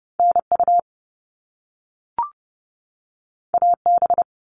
Morse code, heard as a keyed tone, about 30 wpm, 700 Hz. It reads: NU AB
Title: Canadian Provinces and Territories 30wpm